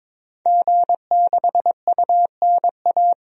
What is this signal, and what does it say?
Morse code, heard as a keyed tone, about 22 wpm, 700 Hz. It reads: G6UNA
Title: DXCC-100 Call Signs - 5 Characters - 22wpm